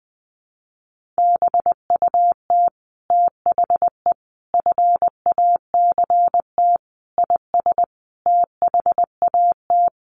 Morse code, heard as a keyed tone, about 20 wpm, 700 Hz. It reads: BUT THE FACT IS THAT
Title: Sets of 5 Words Encore2 20wpm